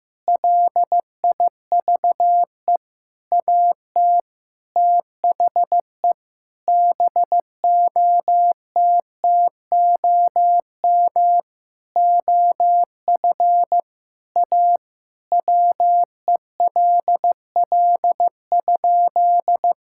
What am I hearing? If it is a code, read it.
Morse code, heard as a keyed tone, about 15 wpm, 700 Hz. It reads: LIVE AT THE BOTTOM OF A WELL?
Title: Sentences from Top 1000 Words - Review Part 6 - 15wpm